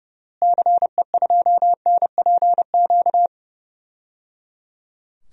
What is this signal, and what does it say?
Morse code, heard as a keyed tone, about 29 wpm, 700 Hz. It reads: CE2NPQ
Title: DXCC-100 Call Signs - 6 Characters - Repeated 3x - 30wpm